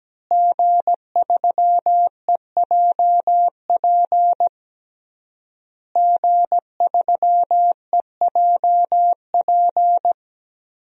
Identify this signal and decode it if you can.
Morse code, heard as a keyed tone, about 17 wpm, 700 Hz. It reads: G3EJP G3EJP
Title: DXCC-100 Call Signs - 5 Characters - Repeated 3x - 17wpm